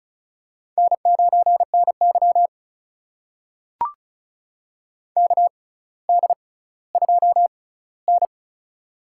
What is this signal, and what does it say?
Morse code, heard as a keyed tone, about 34 wpm, spaced as for 18 wpm, 700 Hz. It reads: N9NY KD2N
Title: USA Call Signs - 4 Characters - 6x Character Spacing - 35wpm